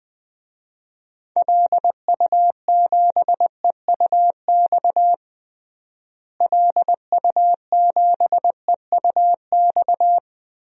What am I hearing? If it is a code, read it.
Morse code, heard as a keyed tone, about 20 wpm, 700 Hz. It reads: LU7EUX LU7EUX
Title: DXCC-100 Call Signs - 6 Characters - Repeated 3x - 20wpm